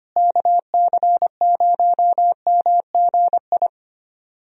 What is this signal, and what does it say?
Morse code, heard as a keyed tone, about 25 wpm, 700 Hz. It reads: KC0MGI
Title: USA Call Signs - 6 Characters - Repeated 3x - 25wpm